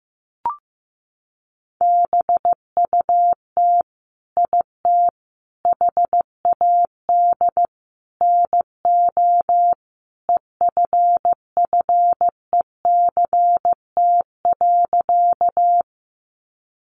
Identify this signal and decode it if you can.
Morse code, heard as a keyed tone, about 15 wpm, 700 Hz. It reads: BUT IT HAD NO EFFECT.
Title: Sentences from Top 1000 Words 15wpm